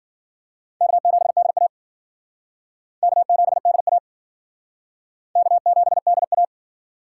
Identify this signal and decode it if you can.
Morse code, heard as a keyed tone, about 54 wpm, 700 Hz. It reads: K7DA K7DA K7DA